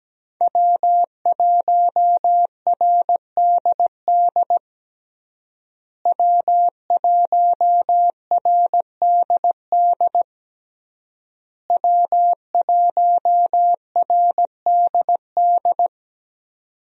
Morse code, heard as a keyed tone, about 17 wpm, 700 Hz. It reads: W1RDD W1RDD W1RDD
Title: USA Call Signs - 5 Characters - Repeated 3x - 17wpm